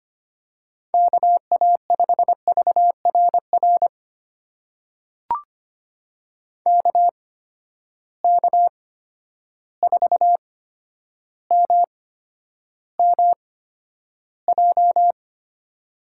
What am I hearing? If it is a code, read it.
Morse code, heard as a keyed tone, about 25 wpm, spaced as for 11 wpm, 700 Hz. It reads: KA5VRR KK4MMJ
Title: USA Call Signs - 6 Characters - 8x Character Spacing - 25wpm